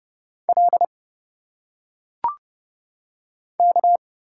Morse code, heard as a keyed tone, about 30 wpm, 700 Hz. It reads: L K
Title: Single Letters - Phonetic Pronunciation 30wpm